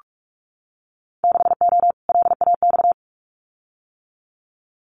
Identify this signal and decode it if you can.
Morse code, heard as a keyed tone, about 46 wpm, 700 Hz. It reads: 6O LAX
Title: Winter Field Day Exchanges Encore 45wpm